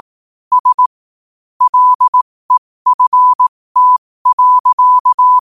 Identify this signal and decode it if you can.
Morse code, heard as a keyed tone, about 18 wpm, spaced as for 15 wpm, 1000 Hz. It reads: S LEFT.
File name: qotdfiles / 15wpm